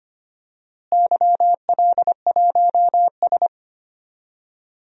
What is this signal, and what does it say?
Morse code, heard as a keyed tone, about 25 wpm, 700 Hz. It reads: YL1S
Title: DXCC-100 Call Signs - 4 Characters - Repeated 2x - 25wpm